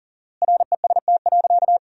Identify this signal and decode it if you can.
Morse code, heard as a keyed tone, about 40 wpm, 700 Hz. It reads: REST.
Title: 4x Word-Spacing - Sentences from Top 500 Words - 40wpm